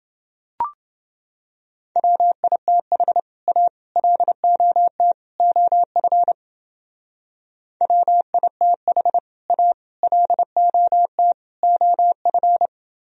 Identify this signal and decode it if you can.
Morse code, heard as a keyed tone, about 29 wpm, 700 Hz. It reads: WITH A LOT OF WITH A LOT OF